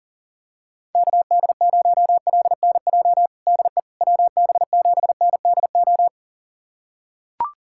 Morse code, heard as a keyed tone, about 40 wpm, 700 Hz. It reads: KD0LNJ DE WB7NDO
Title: USA Callsign DE Callsign 40wpm